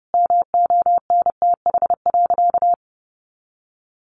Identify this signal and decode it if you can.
Morse code, heard as a keyed tone, about 30 wpm, 700 Hz. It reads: MONTH.
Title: Easy Phrases 30wpm